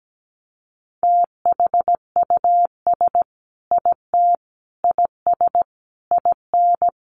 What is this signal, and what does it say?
Morse code, heard as a keyed tone, about 17 wpm, 700 Hz. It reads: THUS IT IS IN
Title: Sentences from Top 800 Words 17wpm